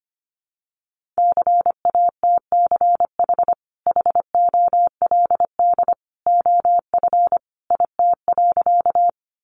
Morse code, heard as a keyed tone, about 25 wpm, 700 Hz. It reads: CATCH HOLD OF IT.
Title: Sentences from Top 1500 Words 25wpm